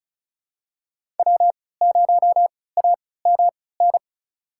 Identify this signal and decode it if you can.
Morse code, heard as a keyed tone, about 34 wpm, spaced as for 20 wpm, 690 Hz. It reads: W0AMN
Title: USA Call Signs - 5 Characters - 3x Character Spacing - 35wpm